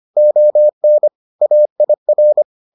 Morse code, heard as a keyed tone, about 25 wpm, 600 Hz. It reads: ON AIR